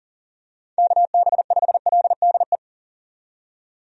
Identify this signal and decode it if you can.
Morse code, heard as a keyed tone, about 40 wpm, 700 Hz. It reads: KB5LDE
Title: USA Call Signs - 6 Characters - 40wpm